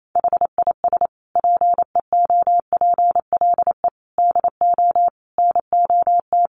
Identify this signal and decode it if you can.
Morse code, heard as a keyed tone, about 28 wpm, 700 Hz. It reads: HIS PEOPLE DO NOT
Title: Sentences from Top 1400 Words 28wpm